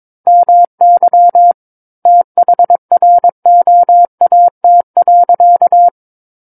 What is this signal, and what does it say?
Morse code, heard as a keyed tone, about 22 wpm, 700 Hz. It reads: MY THROAT.
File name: Book One / 13-Chapter 13 - 22 WPM